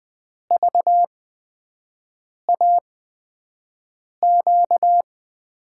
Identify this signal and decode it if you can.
Morse code, heard as a keyed tone, about 20 wpm, spaced as for 5 wpm, 700 Hz. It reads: VAQ
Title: USA Call Signs - 8x Character Spacing - 20wpm